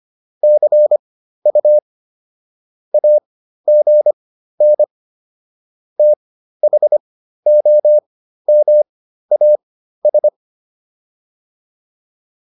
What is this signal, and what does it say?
Morse code, heard as a keyed tone, about 25 wpm, spaced as for 13 wpm, 600 Hz. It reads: CU AGN THOMAS